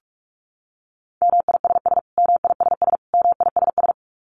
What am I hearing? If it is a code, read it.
Morse code, heard as a keyed tone, about 44 wpm, 700 Hz. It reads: MISS MISS MISS
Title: Top 500 Words Repeated 45wpm